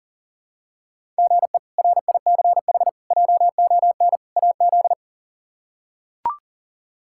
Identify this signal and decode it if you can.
Morse code, heard as a keyed tone, about 37 wpm, 700 Hz. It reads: GE RICH JON AZ
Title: K1USN SST QSO Practice - 40wpm